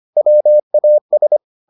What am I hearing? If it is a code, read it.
Morse code, heard as a keyed tone, about 25 wpm, 600 Hz. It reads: WAS